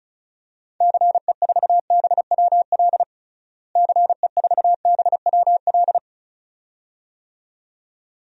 Morse code, heard as a keyed tone, about 36 wpm, 700 Hz. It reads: CE4BWL CE4BWL
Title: DXCC-100 Call Signs - 6 Characters - Repeated 2x - 35wpm